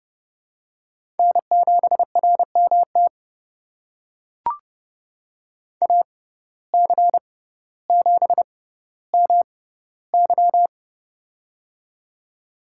Morse code, heard as a keyed tone, about 32 wpm, spaced as for 16 wpm, 700 Hz. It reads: N7RMT AC7MY
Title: USA Call Signs - 5 Characters - 6x Character Spacing - 30wpm